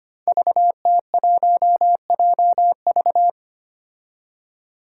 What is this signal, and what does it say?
Morse code, heard as a keyed tone, about 25 wpm, 700 Hz. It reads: VT1JV
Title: Five Challenging Letter-Number Combinations 25wpm